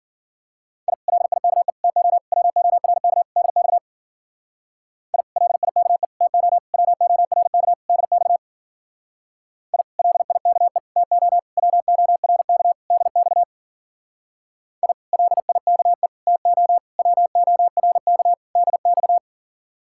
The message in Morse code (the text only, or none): I LIKE TO WORK DX I LIKE TO WORK DX I LIKE TO WORK DX I LIKE TO WORK DX